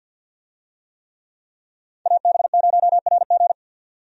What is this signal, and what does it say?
Morse code, heard as a keyed tone, about 51 wpm, 700 Hz. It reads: AB0RG
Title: USA Call Signs - 5 Characters - 50wpm